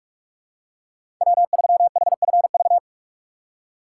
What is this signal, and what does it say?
Morse code, heard as a keyed tone, about 46 wpm, 700 Hz. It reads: W3HFV
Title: USA Call Signs - 6x Character Spacing - 45wpm